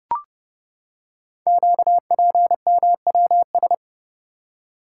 Morse code, heard as a keyed tone, about 30 wpm, 700 Hz. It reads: QPMWS
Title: Code Groups - 5 Letters - 30wpm